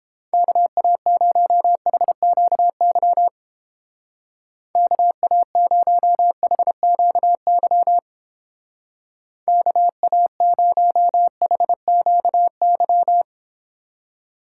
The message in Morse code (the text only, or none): KA0HQY KA0HQY KA0HQY